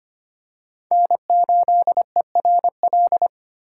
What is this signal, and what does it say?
Morse code, heard as a keyed tone, about 25 wpm, 700 Hz. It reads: N8ERL
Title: USA Call Signs - 5 Characters - 3x Character Spacing - 25wpm